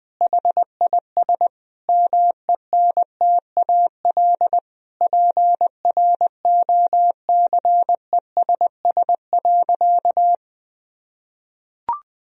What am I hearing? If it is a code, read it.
Morse code, heard as a keyed tone, about 21 wpm, 700 Hz. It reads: HIS MENTAL PROCESS.
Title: Sentences from Top 1000 Words - Review Part 4 - 20wpm